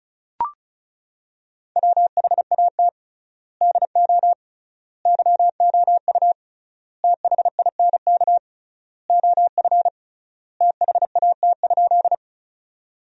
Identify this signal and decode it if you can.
Morse code, heard as a keyed tone, about 34 wpm, 700 Hz. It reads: WHAT DO YOU THINK OF THAT?
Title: 3x Word-Spacing - Sentences from Top 200 Words - 35wpm